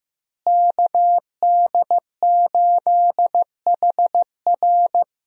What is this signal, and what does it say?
Morse code, heard as a keyed tone, about 15 wpm, 700 Hz. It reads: KD8HR
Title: USA Call Signs - 5 Characters - Repeated 3x - 15wpm